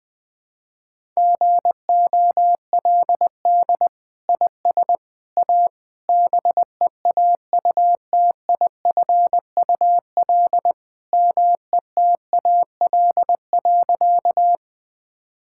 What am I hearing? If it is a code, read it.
Morse code, heard as a keyed tone, about 21 wpm, 700 Hz. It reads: GOLD IS A BEAUTIFUL METAL.
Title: Sentences from Top 1000 Words - Review Part 6 - 20wpm